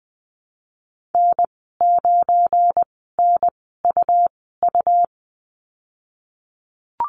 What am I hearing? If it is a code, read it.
Morse code, heard as a keyed tone, about 20 wpm, spaced as for 15 wpm, 700 Hz. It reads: N9NUU
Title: 200 Lesson 32g - Letter k - Call Signs 20wpm